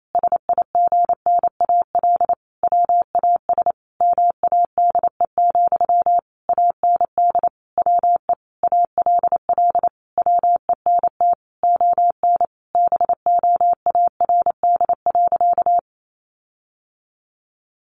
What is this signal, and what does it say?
Morse code, heard as a keyed tone, about 28 wpm, 700 Hz. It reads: SIGNAL WAS MADE, AND WE ALL WENT ON BOARD.
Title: Sentences from Top 1400 Words 28wpm